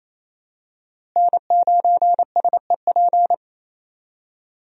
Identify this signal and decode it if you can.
Morse code, heard as a keyed tone, about 28 wpm, 700 Hz. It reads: N9SEP
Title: USA Call Signs - 5 Characters - Repeated 2x - 28wpm